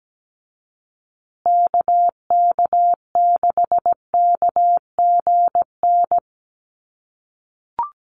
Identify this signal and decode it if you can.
Morse code, heard as a keyed tone, about 17 wpm, 700 Hz. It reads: KK6KGN
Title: USA Call Signs Repeated 3x 17wpm